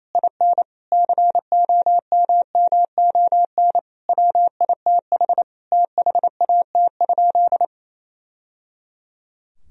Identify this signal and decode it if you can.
Morse code, heard as a keyed tone, about 28 wpm, 700 Hz. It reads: IN COMMON WITH THAT?
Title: Sentences from Top 1000 Words - Review Part 4 - 28wpm